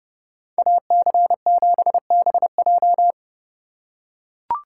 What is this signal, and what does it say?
Morse code, heard as a keyed tone, about 30 wpm, 700 Hz. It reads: AC7BJ